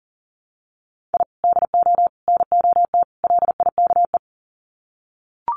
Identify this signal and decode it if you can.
Morse code, heard as a keyed tone, about 40 wpm, 700 Hz. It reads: I DO NOT LIKE
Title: Sets of 4 Words 40wpm